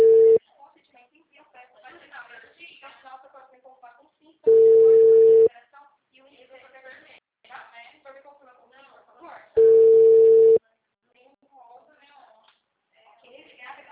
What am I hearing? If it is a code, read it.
Ringback tone (China, 450 Hz), with speech.
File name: out- -2009- - - .98516